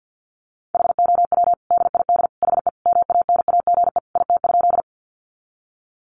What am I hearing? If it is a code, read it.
Morse code, heard as a keyed tone, about 50 wpm, 700 Hz. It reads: HOW DID HE MANAGE IT?
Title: Sentences from Top 1400 Words 50wpm